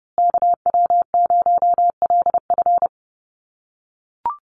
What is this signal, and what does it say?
Morse code, heard as a keyed tone, about 30 wpm, 700 Hz. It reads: KW0LF
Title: USA Call Signs Repeated 3x 30wpm